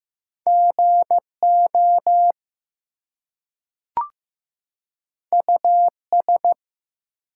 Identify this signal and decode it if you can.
Morse code, heard as a keyed tone, about 15 wpm, 700 Hz. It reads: GO US
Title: Two Letter Words - 15wpm